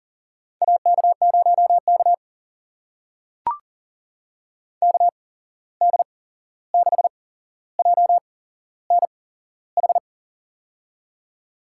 Morse code, heard as a keyed tone, about 40 wpm, spaced as for 15 wpm, 700 Hz. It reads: AK0K KD6JNH
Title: USA Call Signs - 8x Character Spacing - 40wpm